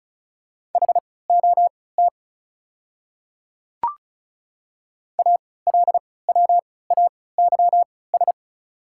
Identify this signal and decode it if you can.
Morse code, heard as a keyed tone, about 34 wpm, spaced as for 20 wpm, 690 Hz. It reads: HOT ALWAYS
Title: 3x Character Spacing - Top 500 Words - 35wpm